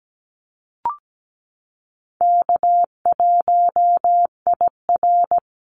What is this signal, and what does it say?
Morse code, heard as a keyed tone, about 17 wpm, 700 Hz. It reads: K1IR
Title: Call Signs Repeated 3x - 17wpm